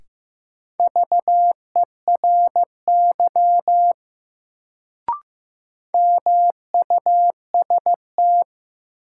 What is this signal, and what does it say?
Morse code, heard as a keyed tone, about 15 wpm, 700 Hz. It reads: VERY MUST
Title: Sets of 2 Words - New Word Drill - 15wpm